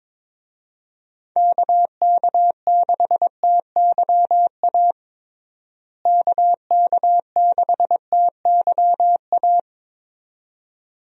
Morse code, heard as a keyed tone, about 21 wpm, 700 Hz. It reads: KK6TYA KK6TYA
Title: USA Call Signs - 6 Characters - Repeated 2x - 22wpm